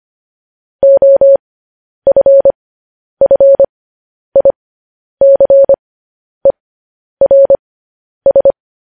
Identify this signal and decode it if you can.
Morse code, heard as a keyed tone, about 25 wpm, spaced as for 10 wpm, 550 Hz. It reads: OFFICERS
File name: podcast / News headlines-1x-25x10